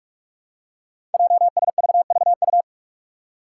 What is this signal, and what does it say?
Morse code, heard as a keyed tone, about 44 wpm, 700 Hz. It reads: JSVVU
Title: Five Challenging Letter-Number Combinations 45wpm